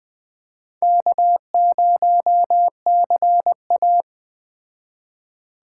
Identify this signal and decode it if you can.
Morse code, heard as a keyed tone, about 21 wpm, 700 Hz. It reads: K0CA
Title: Speed Racing - USA 1x2 Call Signs 17wpm